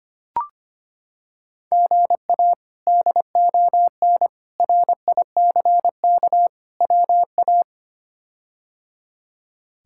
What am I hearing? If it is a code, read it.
Morse code, heard as a keyed tone, about 25 wpm, 700 Hz. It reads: GA DON RICK WA
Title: K1USN SST QSO Practice - 25wpm